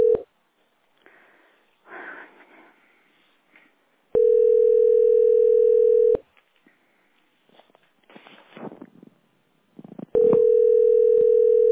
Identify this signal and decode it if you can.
Ringback tone (North America), 440 + 480 Hz.